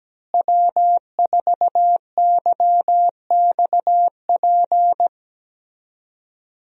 Morse code, heard as a keyed tone, about 17 wpm, 700 Hz. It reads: W4YXP
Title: USA Call Signs - 5 Characters - Repeated 3x - 17wpm